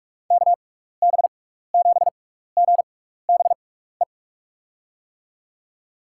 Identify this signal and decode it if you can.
Morse code, heard as a keyed tone, about 44 wpm, spaced as for 15 wpm, 700 Hz. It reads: KB7GBE
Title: USA Call Signs - 6 Characters - 6x Character Spacing - 45wpm